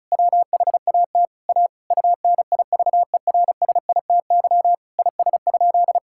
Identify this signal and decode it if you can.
Morse code, heard as a keyed tone, about 36 wpm, 700 Hz. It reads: WHAT A UNIVERSITY IS?
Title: Sentences from Top 1000 Words - Review Part 5 - 35wpm